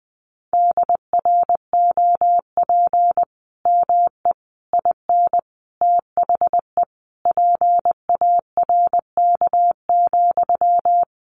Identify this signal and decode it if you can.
Morse code, heard as a keyed tone, about 20 wpm, 700 Hz. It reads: DROP ME IN THE PARK,
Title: Sentences from Top 1400 Words 20wpm